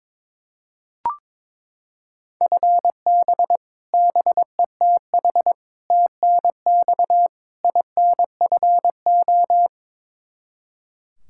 Morse code, heard as a keyed tone, about 22 wpm, 700 Hz. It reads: FB BETH TNX INFO
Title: Practice QSO - Exchange 3 - More Info - 22wpm